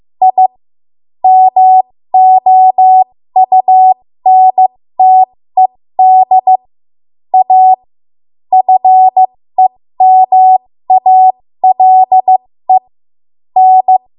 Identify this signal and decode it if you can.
Morse code, heard as a keyed tone, about 15 wpm, spaced as for 13 wpm, 750 Hz. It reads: I MOUNTED A FEMALE N
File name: Morse / 13WPM